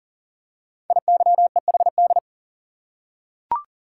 Morse code, heard as a keyed tone, about 38 wpm, 690 Hz. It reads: IYEHD
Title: Code Groups - 5 Letters - 40wpm